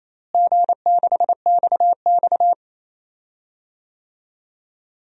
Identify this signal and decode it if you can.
Morse code, heard as a keyed tone, about 28 wpm, 700 Hz. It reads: G6XX
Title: DXCC-100 Call Signs - 4 Characters - Repeated 2x - 28wpm